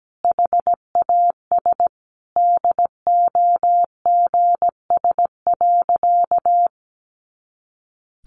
Morse code, heard as a keyed tone, about 17 wpm, 700 Hz. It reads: HAS DOGS.
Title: Additional Easy Phrases 17wpm